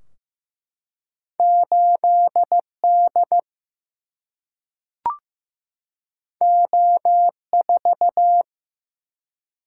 Morse code, heard as a keyed tone, about 15 wpm, 700 Hz. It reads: 8D O4
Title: Single-Letter Single-Number 15wpm